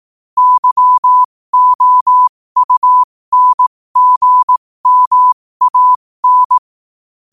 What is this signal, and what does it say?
Morse code, heard as a keyed tone, about 18 wpm, spaced as for 15 wpm, 1000 Hz. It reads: YOUNGMAN
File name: qotdfiles / 15wpm